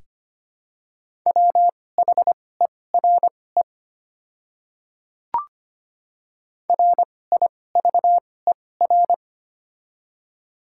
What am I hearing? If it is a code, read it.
Morse code, heard as a keyed tone, about 25 wpm, spaced as for 18 wpm, 700 Hz. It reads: WHERE RIVER
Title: 2x Character Spacing - Top 300 Words - 25wpm